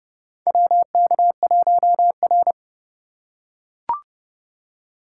Morse code, heard as a keyed tone, about 31 wpm, 700 Hz. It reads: WK1R
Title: USA Call Signs - 2x Character Spacing - 30wpm